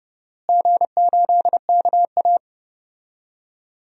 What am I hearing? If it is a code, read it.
Morse code, heard as a keyed tone, about 30 wpm, 700 Hz. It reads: G8KA
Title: DXCC-100 Call Signs - 4 Characters - Repeated 3x - 30wpm